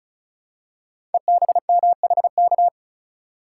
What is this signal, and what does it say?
Morse code, heard as a keyed tone, about 35 wpm, 700 Hz. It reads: EBMHK
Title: Code Groups - 5 Letters - 35wpm